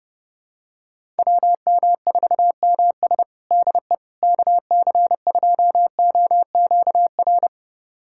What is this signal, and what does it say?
Morse code, heard as a keyed tone, about 32 wpm, 700 Hz. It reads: WM4MS DE KC2OQR
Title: USA Callsign DE Callsign 30wpm